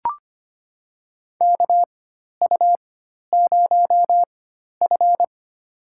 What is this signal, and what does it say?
Morse code, heard as a keyed tone, about 25 wpm, spaced as for 12 wpm, 700 Hz. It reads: KU0F
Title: USA Call Signs - 4x Character Spacing - 25wpm